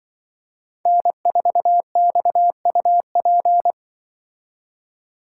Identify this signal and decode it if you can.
Morse code, heard as a keyed tone, about 24 wpm, 700 Hz. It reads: N4XUP
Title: Speed Racing - USA 1x3 Call Signs 20wpm